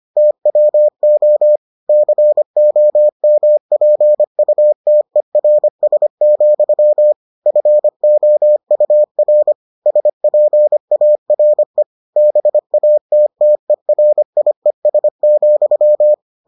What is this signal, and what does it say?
Morse code, heard as a keyed tone, about 25 wpm, 600 Hz. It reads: TWO COMPUTERS, FOUR SPARE BATTERIES,